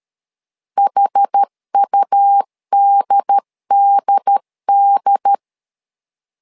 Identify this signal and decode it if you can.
Morse code, heard as a keyed tone, about 13 wpm, 800 Hz. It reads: HUDDD
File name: LEZIONI / 2 UHDN / 2 UHD 12wpm 1540